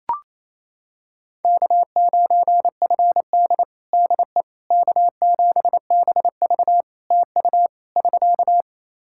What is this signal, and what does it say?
Morse code, heard as a keyed tone, about 27 wpm, 700 Hz. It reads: K9FD DE K7BV TU <SK>
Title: Practice QSO - Exchange 4 - Ending - 28wpm